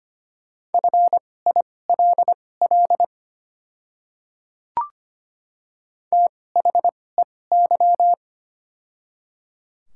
Morse code, heard as a keyed tone, about 25 wpm, spaced as for 18 wpm, 700 Hz. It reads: FILL THEY
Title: 2x Character Spacing - Top 500 Words - 25wpm